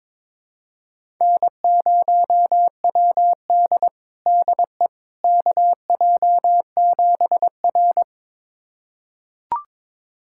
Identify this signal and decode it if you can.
Morse code, heard as a keyed tone, about 22 wpm, 700 Hz. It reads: N0WD DE KJ7R
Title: USA Callsign DE Callsign - 4 Characters - 22wpm